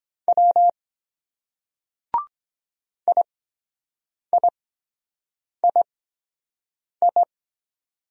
Morse code, heard as a keyed tone, about 23 wpm, spaced as for 7 wpm, 700 Hz. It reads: W IIII